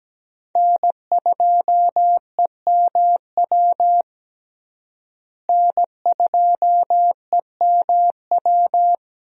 Morse code, heard as a keyed tone, about 17 wpm, 700 Hz. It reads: N2EMW N2EMW
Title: USA Call Signs - 5 Characters - Repeated 2x - 17wpm